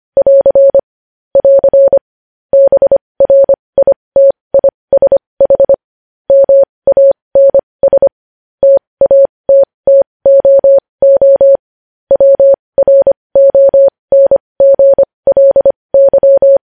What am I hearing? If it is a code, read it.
Morse code, heard as a keyed tone, about 25 wpm, spaced as for 20 wpm, 550 Hz. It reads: <AR> <AR> BRITISH MANS TATTOO WRONGLY
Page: news headlines in morse code at 20 wpm